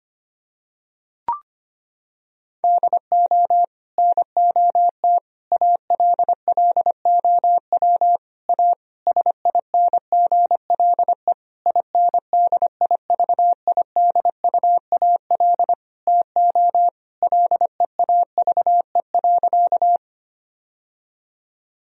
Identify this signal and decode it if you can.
Morse code, heard as a keyed tone, about 25 wpm, 700 Hz. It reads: DO NOT ALLOW A SINGLE INDIVIDUAL TO LEAVE.
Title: Sentences from Top 1000 Words - Review Part 4 - 25wpm